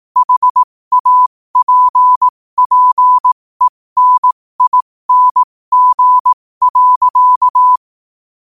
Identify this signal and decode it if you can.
Morse code, heard as a keyed tone, about 18 wpm, spaced as for 15 wpm, 1000 Hz. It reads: HAPPENING.
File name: qotdfiles / 15wpm